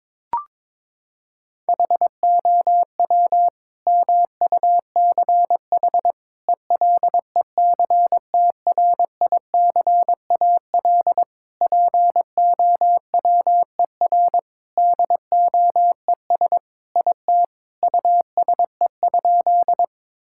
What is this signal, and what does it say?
Morse code, heard as a keyed tone, about 22 wpm, 700 Hz. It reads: HOW MUCH ELECTRICAL POWER DOES IT USE?
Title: Sentences from Top 1000 Words - Review Part 6 - 22wpm